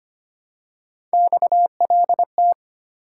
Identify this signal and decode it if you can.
Morse code, heard as a keyed tone, about 25 wpm, 700 Hz. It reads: XLT
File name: Three Letter-Number Combinations - 25wpm